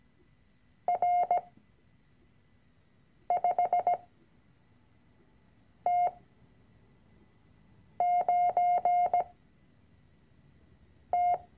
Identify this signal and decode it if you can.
Morse code, heard as a keyed tone, about 17 wpm, spaced as for 4 wpm, 710 Hz. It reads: R5T9T
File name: CW-Online-Kurs / Anfänger / Lektion 006